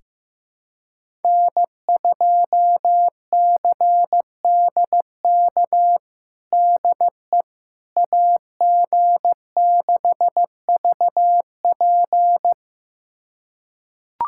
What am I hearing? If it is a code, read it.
Morse code, heard as a keyed tone, about 15 wpm, 700 Hz. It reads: N2CDK DE AG6VP
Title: USA Callsign DE Callsign - 5 Characters - 15wpm